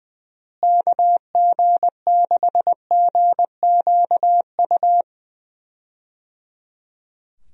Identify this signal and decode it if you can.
Morse code, heard as a keyed tone, about 19 wpm, 700 Hz. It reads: KG6GQU
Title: USA Call Signs - 6 Characters - Repeated 3x - 20wpm